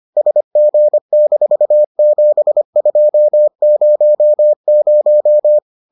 Morse code, heard as a keyed tone, about 25 wpm, 600 Hz. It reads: SG-7200